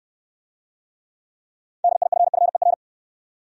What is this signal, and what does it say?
Morse code, heard as a keyed tone, about 64 wpm, 700 Hz. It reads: DIFFER